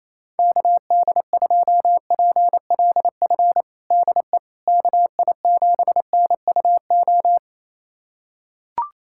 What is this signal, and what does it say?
Morse code, heard as a keyed tone, about 27 wpm, 700 Hz. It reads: KD2PLF DE KI7NUO
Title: USA Callsign DE Callsign - 6 Characters - 28wpm